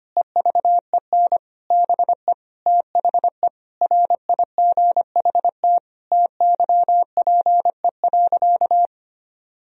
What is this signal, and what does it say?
Morse code, heard as a keyed tone, about 25 wpm, 700 Hz. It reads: EVEN BE THE RIGHT TYPE.
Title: Sentences from Top 1000 Words - Review Part 3 - 25wpm